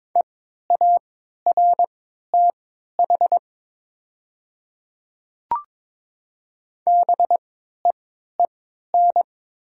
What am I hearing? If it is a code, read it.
Morse code, heard as a keyed tone, about 22 wpm, spaced as for 13 wpm, 700 Hz. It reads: EARTH BEEN
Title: 3x Character Spacing - Top 300 Words - 22wpm